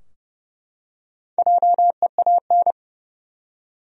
Morse code, heard as a keyed tone, about 30 wpm, 700 Hz. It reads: JEAN
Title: USA Common Names - Max 4 Letters - 30wpm